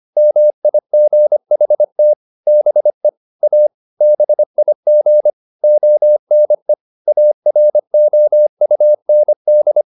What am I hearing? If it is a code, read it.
Morse code, heard as a keyed tone, about 25 wpm, 600 Hz. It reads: MIGHT BE A BIG ONE AROUND
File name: the old man and the sea part 1 25wpm